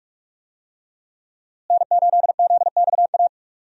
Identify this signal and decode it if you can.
Morse code, heard as a keyed tone, about 44 wpm, 700 Hz. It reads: N8ZXA
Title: USA Call Signs - 5 Characters - Repeated 3x - 45wpm